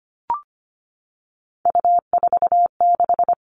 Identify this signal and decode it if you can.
Morse code, heard as a keyed tone, about 25 wpm, 700 Hz. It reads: U46
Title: All Cut Numbers - 3-Digit 25wpm